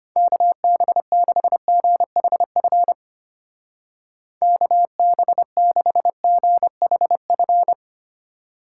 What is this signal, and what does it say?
Morse code, heard as a keyed tone, about 31 wpm, 700 Hz. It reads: KB6GHF KB6GHF
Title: Speed Racing - USA 2x3 Call Signs 25wpm